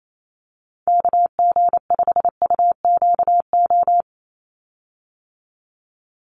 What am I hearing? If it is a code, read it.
Morse code, heard as a keyed tone, about 28 wpm, 700 Hz. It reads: KG5UQO
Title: USA Call Signs 28wpm